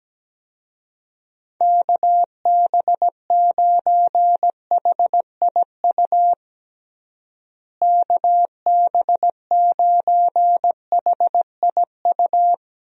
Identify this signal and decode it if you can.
Morse code, heard as a keyed tone, about 17 wpm, 700 Hz. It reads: KB9HIU KB9HIU
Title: USA Call Signs - 6 Characters - Repeated 3x - 17wpm